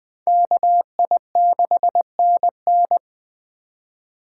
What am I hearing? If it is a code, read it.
Morse code, heard as a keyed tone, about 20 wpm, 700 Hz. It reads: KI6NN
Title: Speed Racing - USA 2x2 Call Signs 17wpm